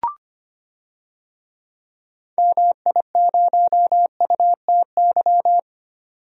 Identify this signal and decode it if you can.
Morse code, heard as a keyed tone, about 25 wpm, 690 Hz. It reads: MI0UTY
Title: DXCC-100 Call Signs - 6 Characters - 25wpm